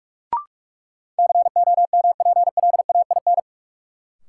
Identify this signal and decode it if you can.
Morse code, heard as a keyed tone, about 41 wpm, 700 Hz. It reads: COMPLAIN
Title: Sentences from Top 1300 Words - New Word Drill Alt - 45wpm